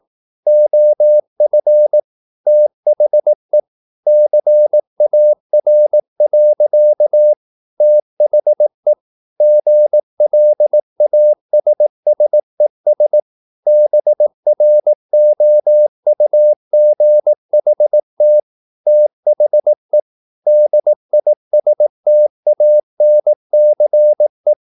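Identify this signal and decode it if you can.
Morse code, heard as a keyed tone, about 18 wpm, 600 Hz. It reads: OF THE CAR. THE GLASSES BROUGHT THE DISTANCE
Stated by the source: Morse practice at 18 wpm, English text, from CW med Gnister.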